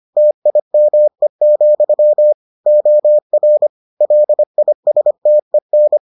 Morse code, heard as a keyed tone, about 25 wpm, 600 Hz. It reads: TIME, OR LISTEN